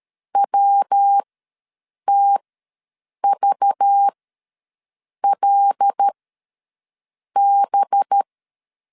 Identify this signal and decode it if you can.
Morse code, heard as a keyed tone, about 13 wpm, spaced as for 6 wpm, 800 Hz. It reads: WTVLB
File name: LEZIONI / 5 LBVC / 5 LBVC ws 12wpm 3159